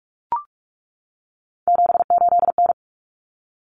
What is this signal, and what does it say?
Morse code, heard as a keyed tone, about 44 wpm, 700 Hz. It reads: 78N
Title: Common Cut Numbers - 3-Digit 45wpm